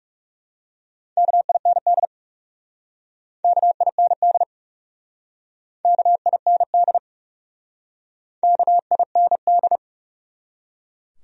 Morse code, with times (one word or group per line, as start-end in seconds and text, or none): 1.2-2.1 KIND
3.4-4.4 KIND
5.8-7.0 KIND
8.4-9.8 KIND